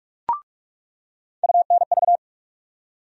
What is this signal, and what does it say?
Morse code, heard as a keyed tone, about 46 wpm, 700 Hz. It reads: UNV
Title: USA Call Sign Postfixes - 3 Letters 45wpm